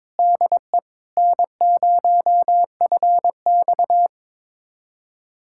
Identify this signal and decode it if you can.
Morse code, heard as a keyed tone, about 21 wpm, 700 Hz. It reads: DE N0FX
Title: USA Callsign DE Callsign - 4 Characters - 22wpm